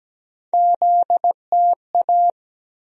Morse code, heard as a keyed tone, about 17 wpm, 700 Hz. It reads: ZTA
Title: USA Call Sign Postfixes - 3 Letters 17wpm